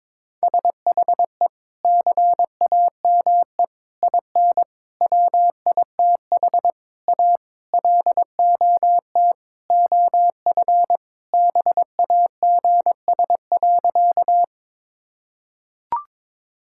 Morse code, heard as a keyed tone, about 22 wpm, 700 Hz. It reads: SHE CAME IN WITH A LOT OF BAGS.
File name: practice / 22wpm / Sentences with 4-Word Phrases - 22wpm